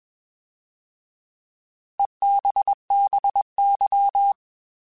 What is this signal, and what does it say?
Morse code, heard as a keyed tone, about 21 wpm, 800 Hz. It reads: EBBY
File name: Alphabetical / Alphabetical mp3s / Es @20